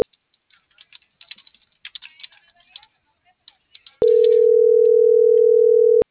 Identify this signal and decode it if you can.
Ringback tone (North America), 440 + 480 Hz.